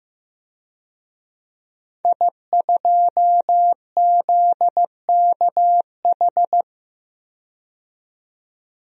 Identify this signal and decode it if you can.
Morse code, heard as a keyed tone, about 15 wpm, 700 Hz. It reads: I2ZKH
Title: DXCC-100 Call Signs - 5 Characters - 15wpm